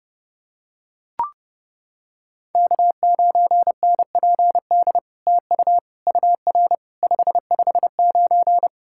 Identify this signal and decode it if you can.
Morse code, heard as a keyed tone, about 30 wpm, 700 Hz. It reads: K9NPD TU UR 559
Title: SOTA QSO Practice 30wpm